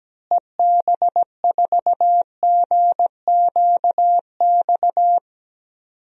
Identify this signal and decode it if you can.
Morse code, heard as a keyed tone, about 17 wpm, 700 Hz. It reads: EB4GQX
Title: DXCC-100 Call Signs - 6 Characters - Repeated 2x - 17wpm